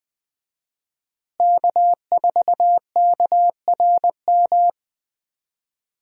Morse code, heard as a keyed tone, about 20 wpm, 700 Hz. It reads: K4KRM
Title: USA Call Signs - 5 Characters - Repeated 3x - 20wpm